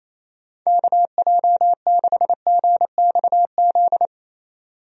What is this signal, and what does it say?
Morse code, heard as a keyed tone, about 28 wpm, 700 Hz. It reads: KJ6GXZ
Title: USA Call Signs - 6 Characters - 28wpm